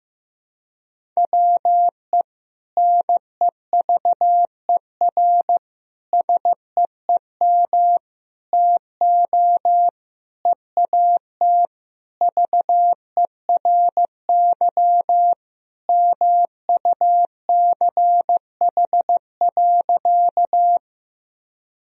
Morse code, heard as a keyed tone, about 15 wpm, 700 Hz. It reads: WE NEVER SEEM TO EAT VERY MUCH.
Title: Sentences from Top 1000 Words - Review Part 1 - 15wpm